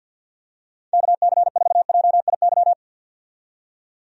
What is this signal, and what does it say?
Morse code, heard as a keyed tone, about 52 wpm, 700 Hz. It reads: KC4JIY
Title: USA Call Signs - 6 Characters - 50wpm